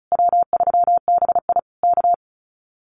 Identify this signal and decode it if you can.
Morse code, heard as a keyed tone, about 34 wpm, 700 Hz. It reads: W3BI K
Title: Common QSO Phrases Encore 35wpm